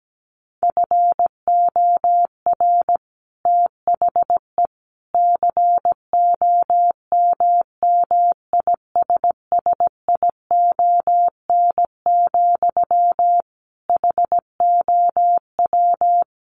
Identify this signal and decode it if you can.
Morse code, heard as a keyed tone, about 17 wpm, 700 Hz. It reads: FOR THE COMMISSION, HOW
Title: Sentences from Top 1400 Words 17wpm